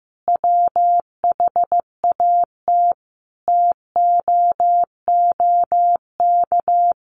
Morse code, heard as a keyed tone, about 14 wpm, 700 Hz. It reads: WHAT TOOK
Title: Sentences from Top 1300 Words 15wpm